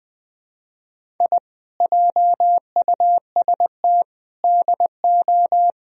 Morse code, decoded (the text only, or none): I JUST DO